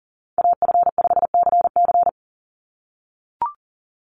Morse code, heard as a keyed tone, about 40 wpm, 700 Hz. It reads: AF5CC
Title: Call Signs - 40wpm